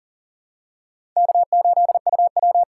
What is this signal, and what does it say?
Morse code, heard as a keyed tone, about 40 wpm, 700 Hz. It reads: K8UW
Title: USA Call Signs - 4 Characters - 4x Character Spacing - 40wpm